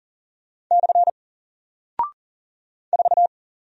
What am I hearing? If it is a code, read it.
Morse code, heard as a keyed tone, about 40 wpm, 700 Hz. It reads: / 4
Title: Single Character (Letter:Number:Punctuation:Prosign) Alt 40wpm